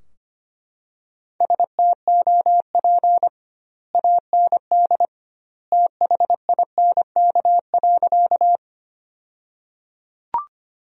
Morse code, heard as a keyed tone, about 25 wpm, 700 Hz. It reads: STOP AND THINK.
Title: 2x Word-Spacing - Sentences from Top 300 Words - 25wpm